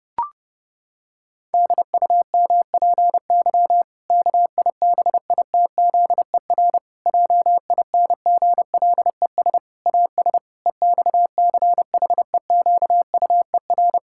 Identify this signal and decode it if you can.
Morse code, heard as a keyed tone, about 30 wpm, 700 Hz. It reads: DUMPY KIBITZER JINGLES AS EXCHEQUER
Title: Pangram Challenge v2 - 30wpm